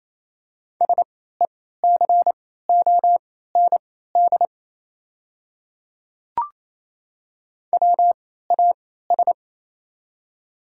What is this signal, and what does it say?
Morse code, heard as a keyed tone, about 28 wpm, spaced as for 16 wpm, 700 Hz. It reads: SECOND WAS
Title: 3x Character Spacing - Top 300 Words - 28wpm